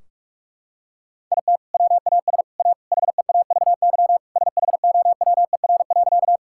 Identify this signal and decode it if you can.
Morse code, heard as a keyed tone, about 41 wpm, 700 Hz. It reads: IT WAS A HEAVY SHOWER.
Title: Sentences from Top 1600 Words 45wpm